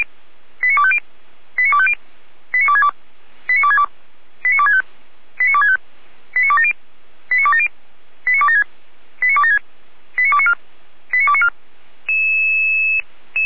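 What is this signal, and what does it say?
Five-tone selective call (ZVEI), heard as quick runs of five tones, tones of about 70 ms. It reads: 89250 89250 89262 89262 89265 89265 89270 89270 89276 89276 89294 89294
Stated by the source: ZVEI five-tone selective call received on SDR.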